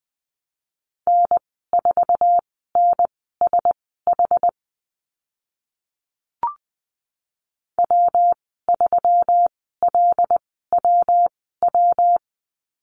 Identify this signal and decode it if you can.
Morse code, heard as a keyed tone, about 20 wpm, spaced as for 15 wpm, 700 Hz. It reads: N4NSH W3LWW
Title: 154 Lesson 25g - Letter p - Call Signs 20wpm